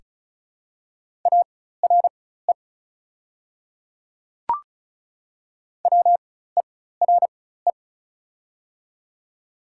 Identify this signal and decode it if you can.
Morse code, heard as a keyed tone, about 34 wpm, spaced as for 16 wpm, 700 Hz. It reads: ARE WERE
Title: 4x Character Spacing - Top 100 Words - 35wpm